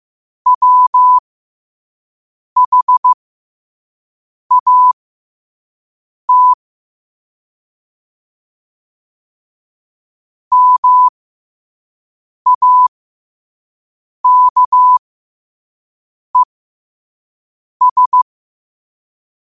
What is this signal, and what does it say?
Morse code, heard as a keyed tone, about 15 wpm, spaced as for 5 wpm, 1000 Hz. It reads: WHAT MAKES
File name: qotdfiles / 5wpm